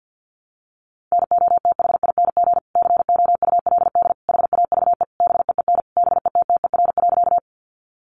Morse code, heard as a keyed tone, about 50 wpm, 700 Hz. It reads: NOTHING COULD HAVE BEEN BETTER.
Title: Sentences from Top 500 Words 50wpm